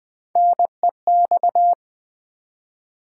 Morse code, heard as a keyed tone, about 20 wpm, 700 Hz. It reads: NEX
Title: Build A Word - 1 Letter At A Time - 4-Letter Words - 20wpm